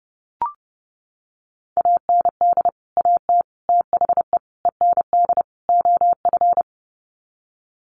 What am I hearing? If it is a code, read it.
Morse code, heard as a keyed tone, about 30 wpm, 700 Hz. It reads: AND AT THE END OF
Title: Sets of 5 Words 30wpm